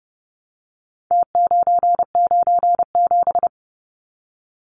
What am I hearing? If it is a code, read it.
Morse code, heard as a keyed tone, about 30 wpm, 700 Hz. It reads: T997
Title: Numbers - Contest Style - 4-Digit 30wpm